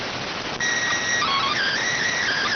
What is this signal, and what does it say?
SSTV (slow-scan television), PD 290.